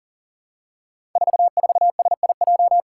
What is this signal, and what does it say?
Morse code, heard as a keyed tone, about 40 wpm, 700 Hz. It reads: 44SIJ
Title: Five Challenging Letter-Number Combinations 40wpm